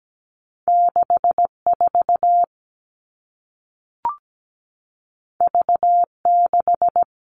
Morse code, heard as a keyed tone, about 17 wpm, 700 Hz. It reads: 64 V6
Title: All Cut Numbers - 2-Digit 17wpm